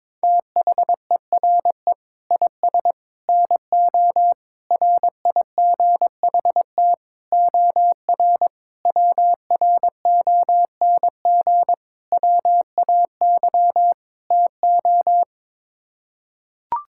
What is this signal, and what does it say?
Morse code, heard as a keyed tone, about 22 wpm, 700 Hz. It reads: THERE IS NO RIGHT OR WRONG WAY TO
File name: practice / 22wpm / Sets of 8 Words - 22wpm